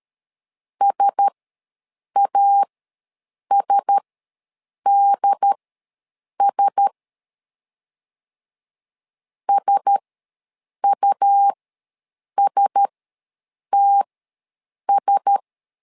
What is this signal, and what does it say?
Morse code, heard as a keyed tone, about 13 wpm, spaced as for 7 wpm, 800 Hz. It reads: SASDS SUSTS
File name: LEZIONI / 4 Q1S ws 12wpm 1827